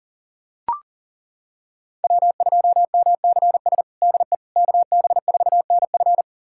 Morse code, heard as a keyed tone, about 40 wpm, 700 Hz. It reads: W2MCS DE KB4NF
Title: USA Callsign DE Callsign - 5 Characters - 40wpm